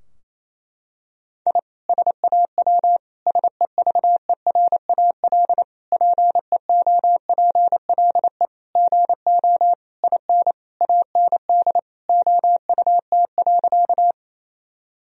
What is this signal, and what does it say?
Morse code, heard as a keyed tone, about 28 wpm, 700 Hz. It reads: I SAW SEVERAL PEOPLE GO IN AND OUT.
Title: Sentences from Top 1000 Words - Review Part 2 - 28wpm